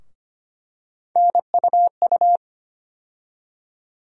Morse code, heard as a keyed tone, about 25 wpm, 700 Hz. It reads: NUU
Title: USA Call Sign Postfixes - 3 Letters 25wpm